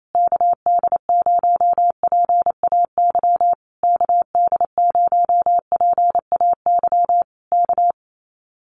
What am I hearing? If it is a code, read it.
Morse code, heard as a keyed tone, about 28 wpm, 700 Hz. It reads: KD0PAY KD0PAY K
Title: Common QSO Phrases Encore 28wpm